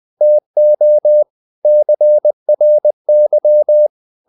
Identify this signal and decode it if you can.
Morse code, heard as a keyed tone, about 20 wpm, 600 Hz. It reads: TO CRY
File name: the old man and the sea part 2 20wpm